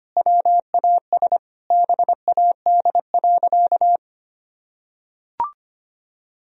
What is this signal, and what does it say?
Morse code, heard as a keyed tone, about 25 wpm, 700 Hz. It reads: WAS BAD.
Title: Sentences from Top 1000 Words - Review Part 1 - 25wpm